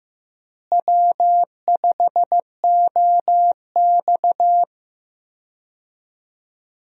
Morse code, heard as a keyed tone, about 15 wpm, 700 Hz. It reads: W5OX
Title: USA Call Signs - 4 Characters - Repeated 2x - 15wpm